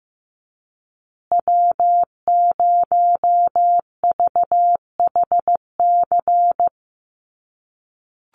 Morse code, heard as a keyed tone, about 15 wpm, 700 Hz. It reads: W0VHC
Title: USA Call Signs Repeated 3x 15wpm